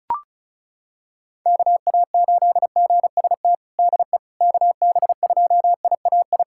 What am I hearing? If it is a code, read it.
Morse code, heard as a keyed tone, about 36 wpm, 700 Hz. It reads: KA8GST DE KB2IAI
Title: USA Callsign DE Callsign - 6 Characters - 35wpm